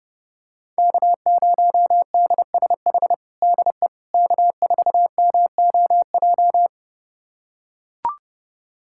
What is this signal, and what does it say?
Morse code, heard as a keyed tone, about 32 wpm, 700 Hz. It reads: K0DSH DE K4MOJ
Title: USA Callsign DE Callsign - 5 Characters - 30wpm